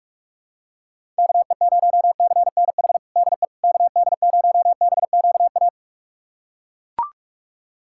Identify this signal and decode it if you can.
Morse code, heard as a keyed tone, about 41 wpm, 700 Hz. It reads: KE0CNH DE KD0BQA
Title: USA Callsign DE Callsign - 6 Characters - 45wpm